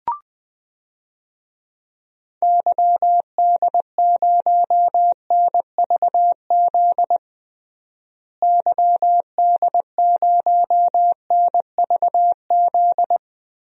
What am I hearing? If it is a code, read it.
Morse code, heard as a keyed tone, about 20 wpm, 700 Hz. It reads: YD0NVZ YD0NVZ
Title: DXCC-100 Call Signs - 6 Characters - Repeated 3x - 20wpm